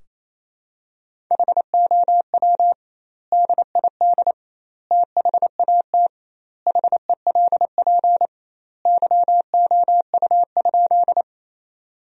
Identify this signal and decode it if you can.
Morse code, heard as a keyed tone, about 28 wpm, 700 Hz. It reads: HOW DID THAT HELP YOU?
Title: 2x Word-Spacing - Sentences from Top 200 Words - 28wpm